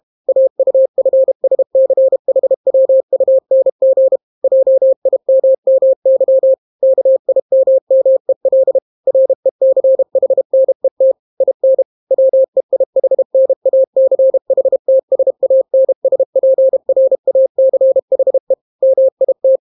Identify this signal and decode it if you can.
Morse code, heard as a keyed tone, about 32 wpm, 530 Hz. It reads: AUFSCHWUNG JIMMY KIMMEL RECHNET IN WEIHNACHTSANSPRACHE MIT